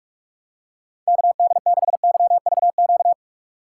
Morse code, heard as a keyed tone, about 41 wpm, 700 Hz. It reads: KD6YVQ
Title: USA Call Signs - 6 Characters - 3x Character Spacing - 45wpm